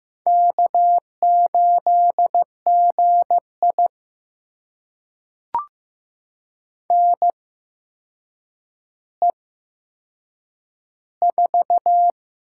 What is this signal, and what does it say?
Morse code, heard as a keyed tone, about 15 wpm, spaced as for 7 wpm, 700 Hz. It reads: K8GI NE4
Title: USA Call Signs - 4 Characters - 8x Character Spacing - 15wpm